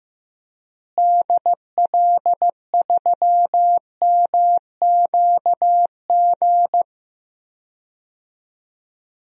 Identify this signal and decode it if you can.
Morse code, heard as a keyed tone, about 16 wpm, 700 Hz. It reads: DL3MQG
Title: DXCC-100 Call Signs - 6 Characters - Repeated 3x - 15wpm